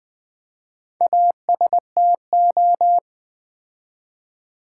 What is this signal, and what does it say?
Morse code, heard as a keyed tone, about 20 wpm, 700 Hz. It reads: ASTO
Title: Build A Word - 4 Letters At A Time - 11-Letter Words - 20wpm